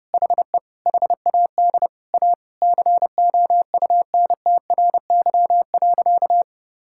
Morse code, heard as a keyed tone, about 30 wpm, 700 Hz. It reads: HE HAD A COUNTRY.
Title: Sentences from Top 1600 Words 30wpm